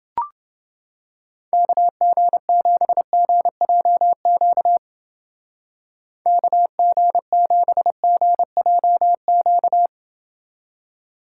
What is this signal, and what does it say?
Morse code, heard as a keyed tone, about 29 wpm, 700 Hz. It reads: KG7GJQ KG7GJQ